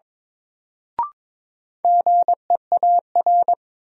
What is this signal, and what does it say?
Morse code, heard as a keyed tone, about 22 wpm, 700 Hz. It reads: GEAR
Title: Sentences from Top 1400 Words - New Word Drill Alt - 22wpm